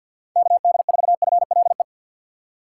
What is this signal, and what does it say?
Morse code, heard as a keyed tone, about 53 wpm, 700 Hz. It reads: KD4FLE
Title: USA Call Signs - 6x Character Spacing - 50wpm